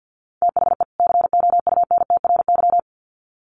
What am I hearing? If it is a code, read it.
Morse code, heard as a keyed tone, about 50 wpm, 700 Hz. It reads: THE COUNTRY
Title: Sets of 2 Words 50wpm